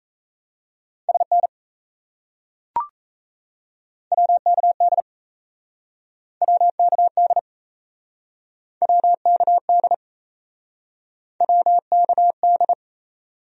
Morse code, heard as a keyed tone, about 37 wpm, 700 Hz. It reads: SN WKD WKD WKD WKD